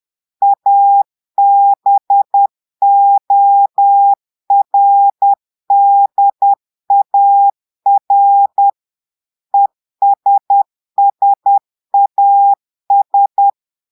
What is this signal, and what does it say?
Morse code, heard as a keyed tone, about 10 wpm, 800 Hz. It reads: ABORDAR ESSAS
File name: portugais texte en clair 2 vitesse 10 mots minutes